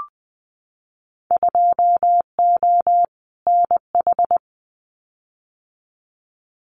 Morse code, heard as a keyed tone, about 20 wpm, 700 Hz. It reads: 2O NH
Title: Winter Field Day Exchanges 20wpm